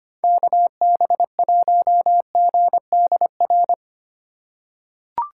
Morse code, heard as a keyed tone, about 25 wpm, 700 Hz. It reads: KB1GDR
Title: USA Call Signs - 6 Characters - 2x Character Spacing - 25wpm